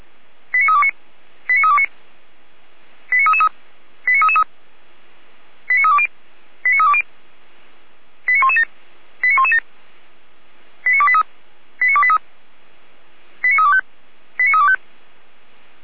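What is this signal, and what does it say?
Five-tone selective call (ZVEI), heard as quick runs of five tones, tones of about 70 ms. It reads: 89329 89329 89333 89333 89320 89320 89117 89117 89283 89283 89325 89325
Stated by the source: Five-tone ZVEI selective call received by SDR.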